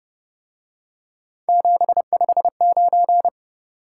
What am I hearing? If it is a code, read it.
Morse code, heard as a keyed tone, about 30 wpm, 700 Hz. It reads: 759
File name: Speed-Racing - Three Letter-Number Combinations - 25wpm